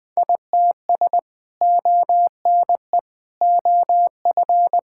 Morse code, heard as a keyed tone, about 20 wpm, 700 Hz. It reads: ITS ONE OF
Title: Sentences with 3-Word Phrases - 20wpm